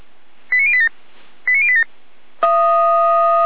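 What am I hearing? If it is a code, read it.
Five-tone selective call (ZVEI), heard as quick runs of five tones, tones of about 70 ms. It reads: 89087 89087
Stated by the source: ZVEI five-tone selective call received with SDR.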